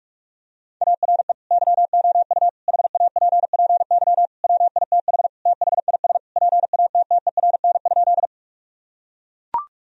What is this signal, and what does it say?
Morse code, heard as a keyed tone, about 41 wpm, 700 Hz. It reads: ARE YOU HAPPY WITH THIS PATTERN?
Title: Sentences from Top 1000 Words - Review Part 2 - 45wpm